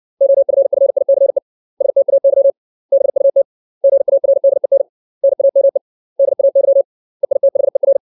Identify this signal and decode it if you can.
Morse code, heard as a keyed tone, about 60 wpm, 550 Hz. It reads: OFFICE STAY BUT GARDEN NAME DAY EITHER
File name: 632 words / 632 Words 60WPM